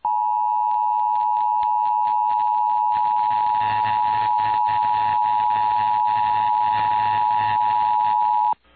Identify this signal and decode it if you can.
Emergency Alert System (SAME) tones (the 853 and 960 Hz attention tone). It is heard from 0.7 to 8.0 s.
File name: eas / KUSO / KUSO